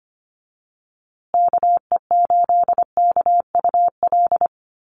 Morse code, heard as a keyed tone, about 25 wpm, 700 Hz. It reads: KE8KUL
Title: USA Call Signs Repeated 3x 25wpm